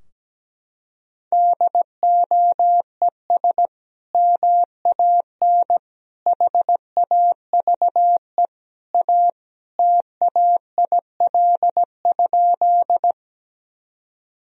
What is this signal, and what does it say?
Morse code, heard as a keyed tone, about 17 wpm, 700 Hz. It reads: DOES MAN HAVE A TAIL?
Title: Sentences from Top 1000 Words - Review Part 3 - 17wpm